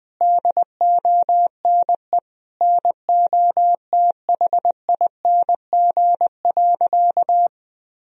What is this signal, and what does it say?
Morse code, heard as a keyed tone, about 21 wpm, 700 Hz. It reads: DONE NOTHING.
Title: Sentences from Top 1000 Words - Review Part 3 - 20wpm